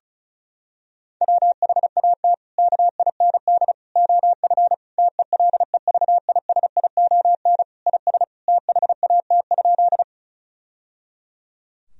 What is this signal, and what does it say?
Morse code, heard as a keyed tone, about 36 wpm, 700 Hz. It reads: WHAT KIND OF TELEVISION IS THAT?
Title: Sentences from Top 1000 Words - Review Part 4 - 35wpm